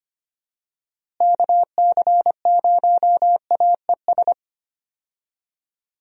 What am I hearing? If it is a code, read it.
Morse code, heard as a keyed tone, about 25 wpm, 700 Hz. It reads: KC0AES
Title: USA Call Signs - 6 Characters - Repeated 3x - 25wpm